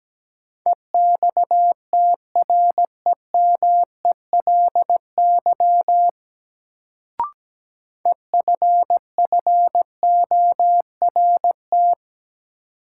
Morse code, heard as a keyed tone, about 17 wpm, 700 Hz. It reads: EXTREMELY EFFORT
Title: Sentences from Top 1200 Words - New Word Drill Alt - 17wpm